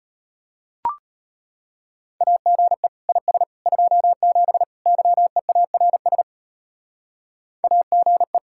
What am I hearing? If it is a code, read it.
Morse code, heard as a keyed tone, about 34 wpm, 700 Hz. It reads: AGE IS 27 YEARS AGE